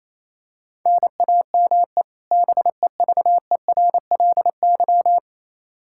Morse code, heard as a keyed tone, about 28 wpm, 700 Hz. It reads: NAME BEVERLY
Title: Practice QSO - Exchange 1 - Beginning - 28wpm